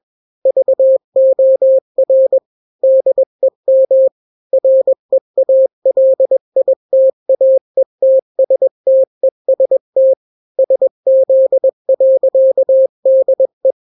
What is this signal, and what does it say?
Morse code, heard as a keyed tone, about 21 wpm, 530 Hz. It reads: VOR DEM REALITAETSTEST SZ.DE
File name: daily / 20wpm